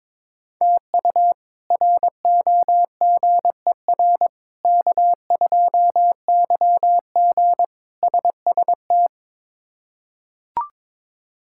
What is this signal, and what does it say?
Morse code, heard as a keyed tone, about 22 wpm, 700 Hz. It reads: TU ROGER K2YG SST
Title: K1USN SST QSO Practice - 22wpm